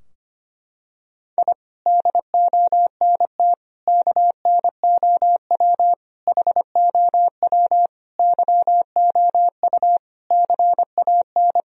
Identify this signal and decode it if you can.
Morse code, heard as a keyed tone, about 25 wpm, 700 Hz. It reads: I DONT KNOW HOW YOU CAN
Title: Sentences from Top 1000 Words - Review Part 2 - 25wpm